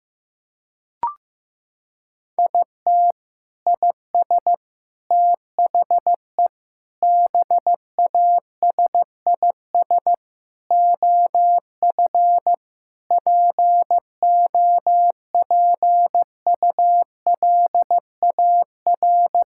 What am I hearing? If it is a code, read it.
Morse code, heard as a keyed tone, about 15 wpm, 700 Hz. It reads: IT IS THE BASIS OF POPULAR
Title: Sentences from Top 1000 Words - Review Part 5 - 15wpm